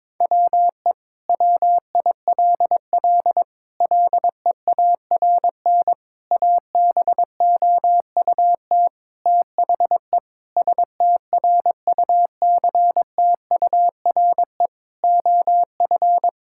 Morse code, heard as a keyed tone, about 22 wpm, 700 Hz. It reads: WE WILL LEARN ABOUT THE STRUCTURE OF
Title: Sentences from Top 1000 Words - Review Part 6 - 22wpm